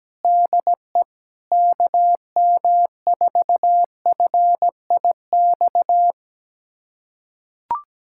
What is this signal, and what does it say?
Morse code, heard as a keyed tone, about 17 wpm, 700 Hz. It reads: DE KM4FIX
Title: USA Callsign DE Callsign 17wpm